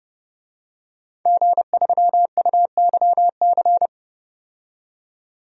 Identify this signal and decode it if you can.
Morse code, heard as a keyed tone, about 30 wpm, 700 Hz. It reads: G3UYC
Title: DXCC-100 Call Signs - 5 Characters - 30wpm